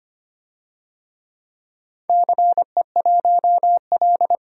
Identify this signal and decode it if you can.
Morse code, heard as a keyed tone, about 25 wpm, 700 Hz. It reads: CE1L
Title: DXCC-100 Call Signs - 4 Characters - Repeated 3x - 25wpm